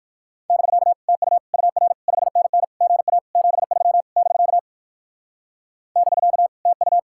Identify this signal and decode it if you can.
Morse code, heard as a keyed tone, about 60 wpm, 700 Hz. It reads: <BK> TU UR 5NN GA 73 <BK> <BK> TU